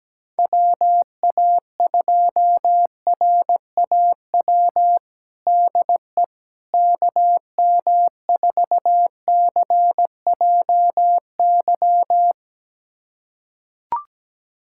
Morse code, heard as a keyed tone, about 17 wpm, 700 Hz. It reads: WA2RAW DE KM4CJY
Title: USA Callsign DE Callsign - 6 Characters - 17wpm